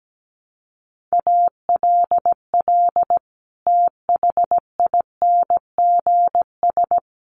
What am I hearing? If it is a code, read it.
Morse code, heard as a keyed tone, about 17 wpm, 700 Hz. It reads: ALL THINGS
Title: Sets of 5 Words Encore 17wpm